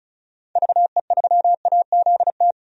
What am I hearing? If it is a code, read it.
Morse code, heard as a keyed tone, about 36 wpm, 700 Hz. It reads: VE3AZT
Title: DXCC-100 Call Signs - 6 Characters - 35wpm